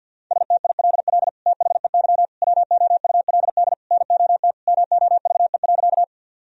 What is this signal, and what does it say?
Morse code, heard as a keyed tone, about 50 wpm, 700 Hz. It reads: STILL THEY WOULD NOT MOVE.
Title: Sentences from Top 1000 Words - Review Part 1 - 50wpm